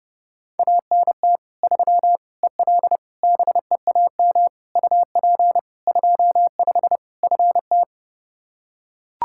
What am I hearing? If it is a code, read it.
Morse code, heard as a keyed tone, about 30 wpm, 700 Hz. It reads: ANT 3 EL BEAM UP 25 FT
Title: Practice QSO - Exchange 2 - Info - 30wpm